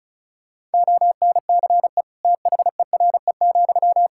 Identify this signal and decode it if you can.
Morse code, heard as a keyed tone, about 36 wpm, 700 Hz. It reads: ONCE THERE,
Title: Sentences from Top 1000 Words - Review Part 6 - 35wpm